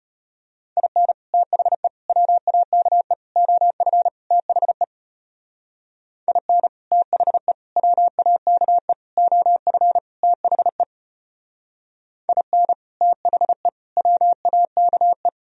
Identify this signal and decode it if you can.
Morse code, heard as a keyed tone, about 29 wpm, 700 Hz. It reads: IN THE WAKE OF THE IN THE WAKE OF THE IN THE WAKE
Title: Speed-Racing - Sets of 5 Words 25wpm